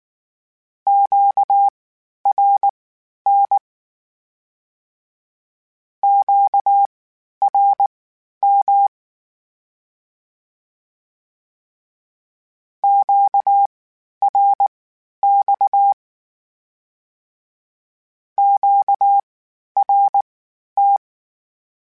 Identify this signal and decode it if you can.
Morse code, heard as a keyed tone, about 19 wpm, spaced as for 11 wpm, 800 Hz. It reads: QRN QRM QRX QRT
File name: (spaced) / Q-Calls Spaced-mp3s / Q Calls sp Once @18